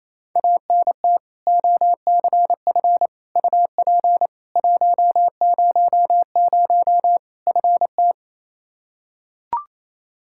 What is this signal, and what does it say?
Morse code, heard as a keyed tone, about 28 wpm, 700 Hz. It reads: ANT OCF UP 100 FT
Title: Practice QSO - Exchange 2 - Info - 28wpm